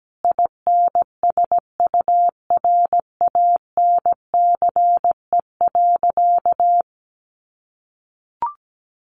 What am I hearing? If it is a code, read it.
Morse code, heard as a keyed tone, about 17 wpm, 700 Hz. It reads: INSURANCE.
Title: A Sound Of Thunder 17wpm